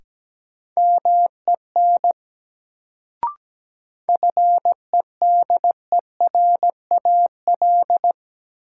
Morse code, heard as a keyed tone, about 17 wpm, 700 Hz. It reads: MEN FEDERAL
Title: Sets of 3 Words - New Word Drill - 17wpm